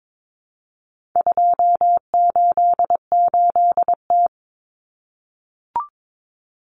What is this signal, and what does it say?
Morse code, heard as a keyed tone, about 22 wpm, 700 Hz. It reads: 288T
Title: Common Cut Numbers - 4-Digit 22wpm